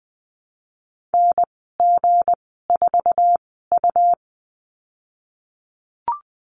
Morse code, heard as a keyed tone, about 20 wpm, spaced as for 15 wpm, 700 Hz. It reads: NG4U
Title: 186 Lesson 30g - Letter b - Call Signs 20wpm